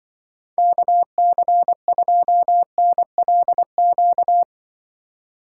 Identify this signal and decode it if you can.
Morse code, heard as a keyed tone, about 24 wpm, 700 Hz. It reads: KC2NLQ
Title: Speed Racing - USA 2x3 Call Signs 20wpm